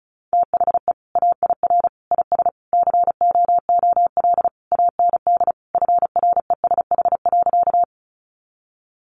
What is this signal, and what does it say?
Morse code, heard as a keyed tone, about 34 wpm, 700 Hz. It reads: THE AIR IS COOL AND FRESH.
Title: Sentences from Top 1200 Words 35wpm